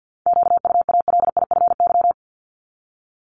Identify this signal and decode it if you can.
Morse code, heard as a keyed tone, about 50 wpm, 700 Hz. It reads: QUALIFY
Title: Top 1751-2000 Words 50wpm